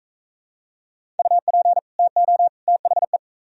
Morse code, heard as a keyed tone, about 43 wpm, 700 Hz. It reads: UP TO THE